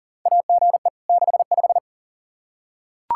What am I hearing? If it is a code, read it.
Morse code, heard as a keyed tone, about 40 wpm, 700 Hz. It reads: AGE 65
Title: K1USN SST QSO Practice 40wpm